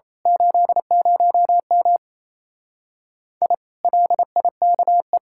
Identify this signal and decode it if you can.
Morse code, heard as a keyed tone, about 26 wpm, 700 Hz. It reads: 80M I LIKE
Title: Speed-Racing Common QSO Phrases - 28wpm